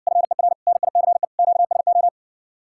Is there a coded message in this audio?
Yes: Morse code from 0.1 to 2.1 s, heard as a keyed tone, about 61 wpm, 700 Hz. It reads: FER NICE QSO